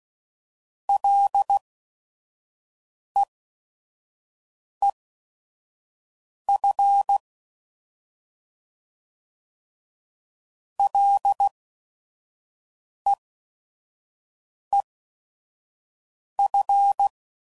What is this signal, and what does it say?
Morse code, heard as a keyed tone, about 16 wpm, spaced as for 5 wpm, 800 Hz. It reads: LEEF LEEF
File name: Mixed / Short Runs / SinglePrefixes